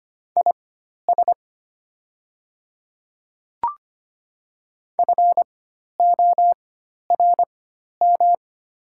Morse code, heard as a keyed tone, about 25 wpm, spaced as for 12 wpm, 700 Hz. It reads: IS FORM
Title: 4x Character Spacing - Top 200 Words - 25wpm